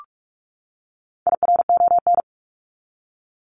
Morse code, heard as a keyed tone, about 44 wpm, 700 Hz. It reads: IRON
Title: Top 1751-2000 Words 45wpm